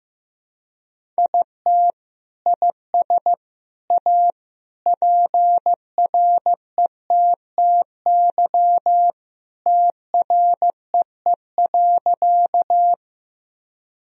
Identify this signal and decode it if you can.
Morse code, heard as a keyed tone, about 15 wpm, 700 Hz. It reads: IT IS A PRETTY TREE.
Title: Sentences from Top 1000 Words - Review Part 1 - 15wpm